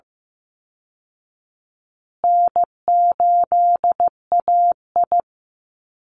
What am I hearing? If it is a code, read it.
Morse code, heard as a keyed tone, about 15 wpm, 700 Hz. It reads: N8AI
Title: Call Signs Repeated 3x - 15wpm